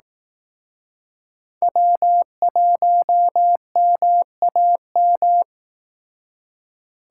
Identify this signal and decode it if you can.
Morse code, heard as a keyed tone, about 18 wpm, 700 Hz. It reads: W1MAM
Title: Speed Racing - USA 1x3 Call Signs 15wpm